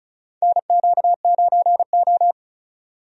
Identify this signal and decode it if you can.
Morse code, heard as a keyed tone, about 36 wpm, 700 Hz. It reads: NQ9O
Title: USA Call Signs - 4 Characters - 35wpm